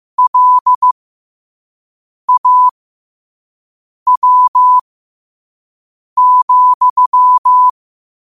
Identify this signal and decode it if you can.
Morse code, heard as a keyed tone, about 15 wpm, spaced as for 5 wpm, 1000 Hz. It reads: LAW,
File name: qotdfiles / 5wpm